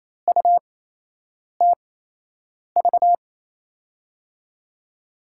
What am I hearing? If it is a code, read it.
Morse code, heard as a keyed tone, about 28 wpm, spaced as for 8 wpm, 700 Hz. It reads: UTV
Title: USA Call Signs - 6 Characters - 8x Character Spacing - 28wpm